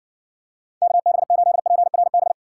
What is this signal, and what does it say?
Morse code, heard as a keyed tone, about 60 wpm, 700 Hz. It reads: KB8PRB